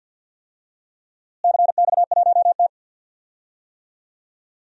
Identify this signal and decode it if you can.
Morse code, heard as a keyed tone, about 47 wpm, 700 Hz. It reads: CX1T
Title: DXCC-100 Call Signs - 4 Characters - Repeated 2x - 50wpm